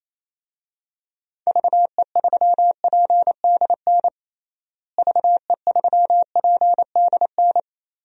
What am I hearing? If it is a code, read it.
Morse code, heard as a keyed tone, about 28 wpm, 700 Hz. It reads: VE3PDN VE3PDN
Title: DXCC-100 Call Signs - 6 Characters - Repeated 3x - 28wpm